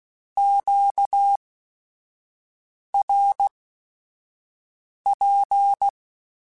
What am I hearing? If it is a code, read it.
Morse code, heard as a keyed tone, about 16 wpm, spaced as for 5 wpm, 800 Hz. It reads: QRP
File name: Q Calls / Q Calls-mp3s / Q Calls Twice @5